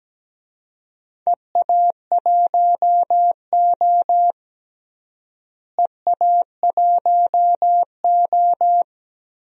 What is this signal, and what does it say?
Morse code, heard as a keyed tone, about 17 wpm, 700 Hz. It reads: EA1O EA1O
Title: DXCC-100 Call Signs - 4 Characters - Repeated 2x - 17wpm